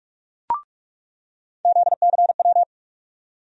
Morse code, heard as a keyed tone, about 46 wpm, 700 Hz. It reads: ZCW
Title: USA Call Sign Postfixes - 3 Letters 45wpm